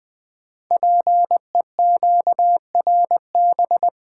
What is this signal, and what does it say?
Morse code, heard as a keyed tone, about 20 wpm, 700 Hz. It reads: PEQRB
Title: Code Groups - 5 Letters - 20wpm